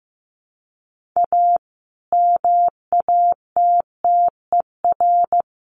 Morse code, heard as a keyed tone, about 15 wpm, 700 Hz. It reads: A MATTER
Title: Sets of 4 Words 15wpm